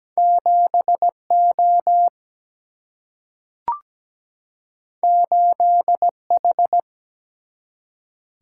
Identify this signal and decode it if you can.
Morse code, heard as a keyed tone, about 17 wpm, 700 Hz. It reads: 7 O 8 H
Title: Single-Letter Single-Number 17wpm